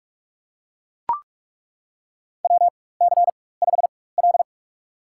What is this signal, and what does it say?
Morse code, heard as a keyed tone, about 42 wpm, spaced as for 21 wpm, 700 Hz. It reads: WC5L
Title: USA Call Signs - 4 Characters - 4x Character Spacing - 45wpm